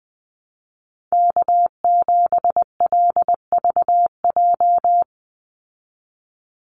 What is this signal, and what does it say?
Morse code, heard as a keyed tone, about 20 wpm, 700 Hz. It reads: K7LVJ
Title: Call Signs Repeated 3x - 20wpm